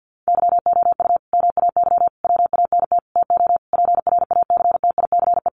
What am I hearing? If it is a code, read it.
Morse code, heard as a keyed tone, about 50 wpm, 700 Hz. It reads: YOU MAY WANT TO PRACTICE
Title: Sentences from Top 700 Words 50wpm